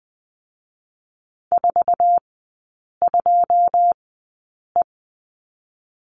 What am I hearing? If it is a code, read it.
Morse code, heard as a keyed tone, about 20 wpm, spaced as for 8 wpm, 700 Hz. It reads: 42E
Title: 127 Lesson 21e - Number 6 - Three Characters 20wpm